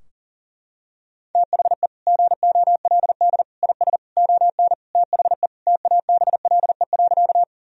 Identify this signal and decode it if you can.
Morse code, heard as a keyed tone, about 40 wpm, 700 Hz. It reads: THE GOLD IS ON THE TABLE.
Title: Sentences from Top 1000 Words - Review Part 3 - 40wpm